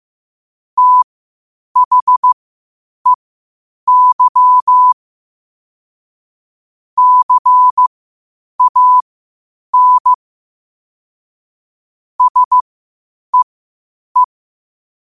Morse code, heard as a keyed tone, about 15 wpm, spaced as for 8 wpm, 1000 Hz. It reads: THEY CAN SEE
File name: qotdfiles / 8wpm